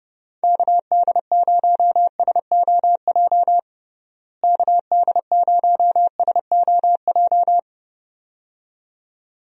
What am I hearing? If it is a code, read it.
Morse code, heard as a keyed tone, about 30 wpm, 700 Hz. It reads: KD0SOJ KD0SOJ
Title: USA Call Signs - 6 Characters - Repeated 2x - 30wpm